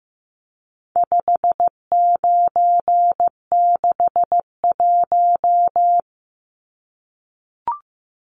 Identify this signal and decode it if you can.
Morse code, heard as a keyed tone, about 15 wpm, 700 Hz. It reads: 5961
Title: Numbers - 4-Digit 15wpm